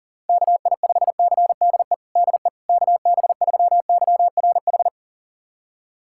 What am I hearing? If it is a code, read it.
Morse code, heard as a keyed tone, about 40 wpm, 700 Hz. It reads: KI5CDE DE KB3YRH
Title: USA Callsign DE Callsign 40wpm